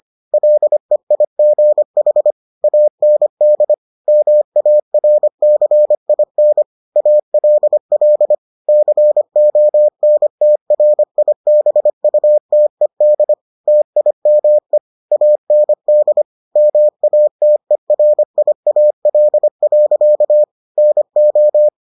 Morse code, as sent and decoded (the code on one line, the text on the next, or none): .-.. .. --. .... / .- -. -.. / -- .- .-. -.-. .. -. / .- .-.. .-.. / -.-. --- -. - .-. .. -... ..- - . -.. / - .. -- . / .- -. -.. / -- .- - . .-. .. .- .-.. .-.-.- / -. ---
LEIGH AND MARCIN ALL CONTRIBUTED TIME AND MATERIAL. NO